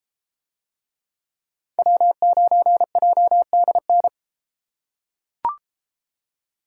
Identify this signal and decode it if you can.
Morse code, heard as a keyed tone, about 33 wpm, 700 Hz. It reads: W9JDN